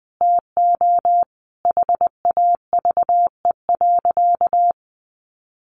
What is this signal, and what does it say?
Morse code, heard as a keyed tone, about 20 wpm, 700 Hz. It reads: TO HAVE.
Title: Sentences from Top 1400 Words 20wpm